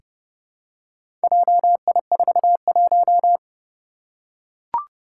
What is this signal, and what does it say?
Morse code, heard as a keyed tone, about 30 wpm, 700 Hz. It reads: JI41
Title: Four Challenging Letter-Number Combinations 30wpm